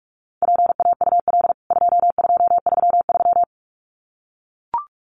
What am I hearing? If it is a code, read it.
Morse code, heard as a keyed tone, about 44 wpm, 700 Hz. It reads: PAUL 2233
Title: CWT Exchange 45wpm